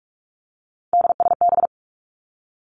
Morse code, heard as a keyed tone, about 46 wpm, 700 Hz. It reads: DSB
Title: Three Challenging Letter-Number Combinations 45wpm